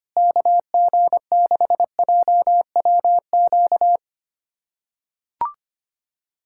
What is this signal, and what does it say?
Morse code, heard as a keyed tone, about 25 wpm, 700 Hz. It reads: KG6JWQ
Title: USA Callsign DE Callsign - 6 Characters - 25wpm